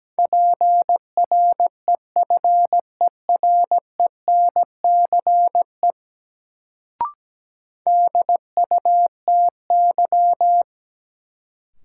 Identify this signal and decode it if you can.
Morse code, heard as a keyed tone, about 17 wpm, 700 Hz. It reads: PREFERENCE DUTY
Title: Sentences from Top 1500 Words - New Word Drill Alt - 17wpm